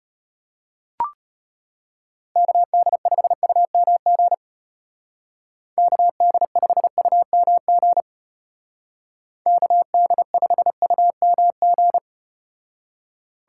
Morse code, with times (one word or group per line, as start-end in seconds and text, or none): 2.4-4.3 KD5UMG
5.8-8.0 KD5UMG
9.5-12.0 KD5UMG